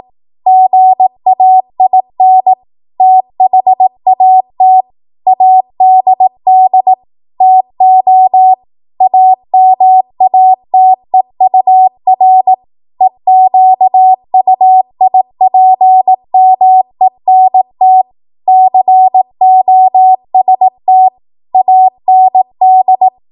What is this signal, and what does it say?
Morse code, heard as a keyed tone, about 18 wpm, 750 Hz. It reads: GAIN THAT ADD TO AMATEUR EQUIPMENT COST AND
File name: Morse / 18WPM